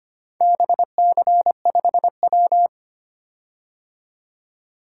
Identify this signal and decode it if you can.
Morse code, heard as a keyed tone, about 25 wpm, 700 Hz. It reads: BC5W
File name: practice / 25wpm / Four Letter-Number Combinations - 25wpm